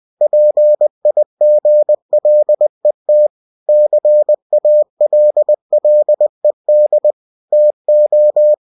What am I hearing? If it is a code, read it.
Morse code, heard as a keyed tone, about 20 wpm, 600 Hz. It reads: PIGLET CALLED TO
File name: cw-books / winnie-the-pooh / winnie-20wpm-page-0021